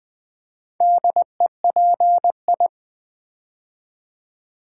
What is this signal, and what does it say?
Morse code, heard as a keyed tone, about 20 wpm, 690 Hz. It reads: DEPI
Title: Build A Word - 2 Letters At A Time - 6-Letter Words - 20wpm